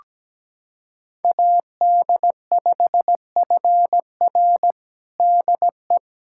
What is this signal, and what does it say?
Morse code, heard as a keyed tone, about 17 wpm, 700 Hz. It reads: AD5FR DE
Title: USA Callsign DE Callsign - 5 Characters - 17wpm